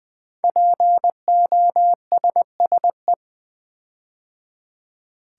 Morse code, heard as a keyed tone, about 20 wpm, 700 Hz. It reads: POSSE
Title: Build A Word - 5 Letters At A Time - 10-Letter Words - 20wpm